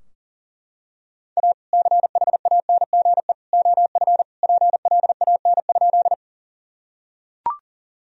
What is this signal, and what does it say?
Morse code, heard as a keyed tone, about 40 wpm, 700 Hz. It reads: A CHANGE OF PLAN?
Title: Sentences from Top 1000 Words - Review Part 2 - 40wpm